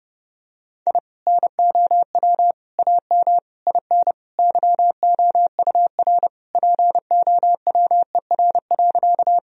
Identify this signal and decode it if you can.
Morse code, heard as a keyed tone, about 30 wpm, 700 Hz. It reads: I NOW AM IN YOUR POWER.
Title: Sentences from Top 1000 Words - Review Part 2 - 30wpm